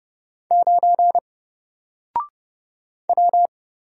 Morse code, heard as a keyed tone, about 30 wpm, 700 Hz. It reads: 9 W
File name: Single-Letter-Number-30wpm